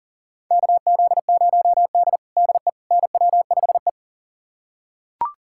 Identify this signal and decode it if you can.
Morse code, heard as a keyed tone, about 37 wpm, 700 Hz. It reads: KZ0D DE NW5E
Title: USA Callsign DE Callsign - 4 Characters - 40wpm